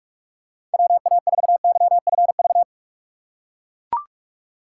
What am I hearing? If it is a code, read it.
Morse code, heard as a keyed tone, about 43 wpm, 700 Hz. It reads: WA4YFV
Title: USA Call Signs - 6 Characters - 2x Character Spacing - 45wpm